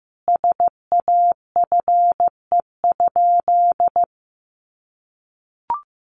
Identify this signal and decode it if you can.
Morse code, heard as a keyed tone, about 14 wpm, 700 Hz. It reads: SAFE?
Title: Sentences from Top 900 Words 15wpm